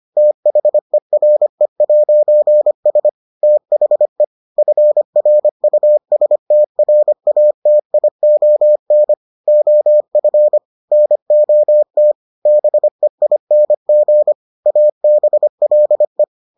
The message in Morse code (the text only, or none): THERE'S THE FRUSTRATION OF NOT BEING ABLE